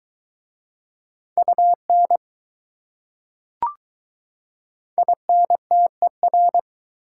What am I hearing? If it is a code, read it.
Morse code, heard as a keyed tone, about 20 wpm, 690 Hz. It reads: UN INTER